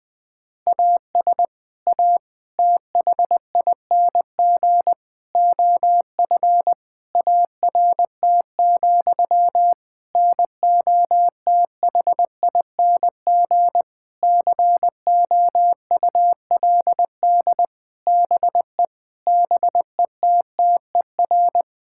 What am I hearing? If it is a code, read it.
Morse code, heard as a keyed tone, about 20 wpm, 700 Hz. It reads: AS A THING OF ART, NOTHING COULD BE BETTER
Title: Sentences from Top 1000 Words - Review Part 6 - 20wpm